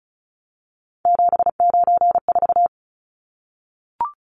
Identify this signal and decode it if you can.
Morse code, heard as a keyed tone, about 34 wpm, 700 Hz. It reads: 794
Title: Numbers - 3-Digit 35wpm